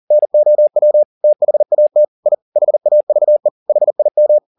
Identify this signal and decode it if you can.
Morse code, heard as a keyed tone, about 40 wpm, 600 Hz. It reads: NOW THAT I HAVE HIM
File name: the old man and the sea 2 40wpm